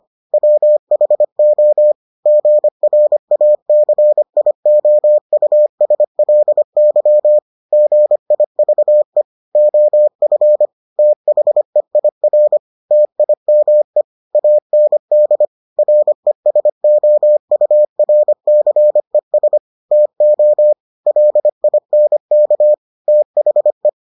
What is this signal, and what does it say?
Morse code, heard as a keyed tone, about 25 wpm, 600 Hz. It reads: WHO GRACIOUSLY GIVE OF THEIR TIME AND RESOURCES TO LINK THE